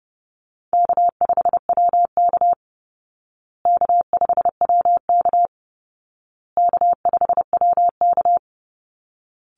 Morse code, heard as a keyed tone, about 30 wpm, 700 Hz. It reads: K5WK K5WK K5WK
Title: Easy Call Signs Repeated 3x - 30wpm